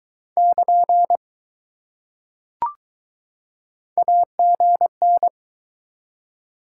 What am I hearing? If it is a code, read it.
Morse code, heard as a keyed tone, about 20 wpm, 700 Hz. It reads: <KN> AGN